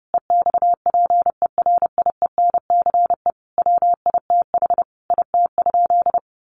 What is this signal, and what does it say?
Morse code, heard as a keyed tone, about 30 wpm, 700 Hz. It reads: EXPERIENCE WITH IT?
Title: Sentences from Top 600 Words 30wpm